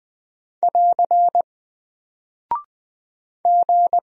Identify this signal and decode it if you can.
Morse code, heard as a keyed tone, about 20 wpm, 700 Hz. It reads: <AR> G
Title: Single Character (Letter:Number:Punctuation:Prosign) Alt 20wpm